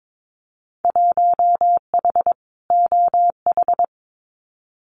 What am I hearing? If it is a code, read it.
Morse code, heard as a keyed tone, about 22 wpm, 700 Hz. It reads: 1H OH
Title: Winter Field Day Exchanges 22wpm